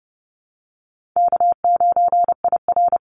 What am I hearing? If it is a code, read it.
Morse code, heard as a keyed tone, about 30 wpm, 700 Hz. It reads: K9IR
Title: NA SOTA Chaser Callsigns 30wpm